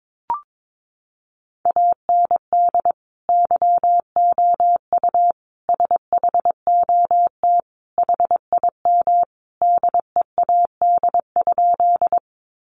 Morse code, heard as a keyed tone, about 22 wpm, 700 Hz. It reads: AND YOU SHOT HIM DEAD?
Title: Sentences from Top 1500 Words 22wpm